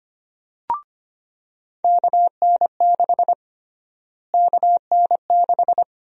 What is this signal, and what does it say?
Morse code, heard as a keyed tone, about 25 wpm, 700 Hz. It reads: KN6 KN6
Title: USA Call Sign Prefixes - Repeated 2x - 25wpm